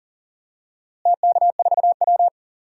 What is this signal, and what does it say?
Morse code, heard as a keyed tone, about 39 wpm, 700 Hz. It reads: TK4W
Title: DXCC-100 Call Signs - 4 Characters - Repeated 2x - 40wpm